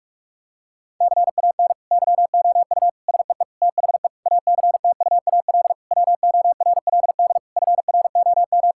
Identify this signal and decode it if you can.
Morse code, heard as a keyed tone, about 46 wpm, 700 Hz. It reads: CAN YOU SEE THE ACTUAL WORLD FROM
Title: Sentences from Top 1000 Words - Review Part 5 - 45wpm